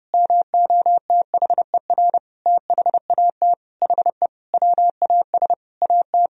Morse code, heard as a keyed tone, about 30 wpm, 700 Hz. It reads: MOTHER THAT HE WAS AT
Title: Sentences from Top 1000 Words - Review Part 1 - 30wpm